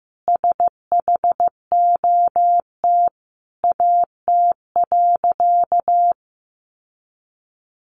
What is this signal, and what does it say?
Morse code, heard as a keyed tone, about 15 wpm, 700 Hz. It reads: SHOT AT.
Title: Sentences from Top 1100 Words 15wpm